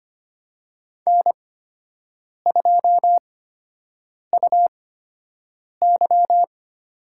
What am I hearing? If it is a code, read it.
Morse code, heard as a keyed tone, about 25 wpm, spaced as for 7 wpm, 700 Hz. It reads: N2UY
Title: USA Call Signs - 8x Character Spacing - 25wpm